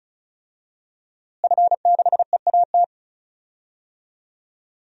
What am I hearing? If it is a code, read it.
Morse code, heard as a keyed tone, about 34 wpm, 700 Hz. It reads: F6EAT
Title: DXCC-100 Call Signs - 5 Characters - Repeated 3x - 35wpm